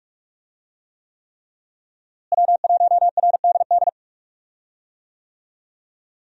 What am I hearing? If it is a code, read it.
Morse code, heard as a keyed tone, about 43 wpm, 700 Hz. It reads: W1RDD
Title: USA Call Signs - 5 Characters - Repeated 3x - 45wpm